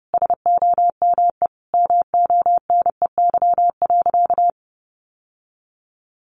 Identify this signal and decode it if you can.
Morse code, heard as a keyed tone, about 30 wpm, 700 Hz. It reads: SOME MONEY.
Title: Sentences from Top 1500 Words 30wpm